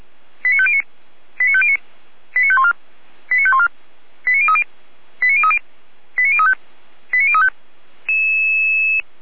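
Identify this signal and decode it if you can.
Five-tone selective call (ZVEI), heard as quick runs of five tones, tones of about 70 ms. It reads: 89509 89509 89514 89514 89030 89030 89035 89035
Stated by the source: ZVEI five-tone call 89099 on channel 455, received by SDR at Feuerwehr Orlamünde fire service.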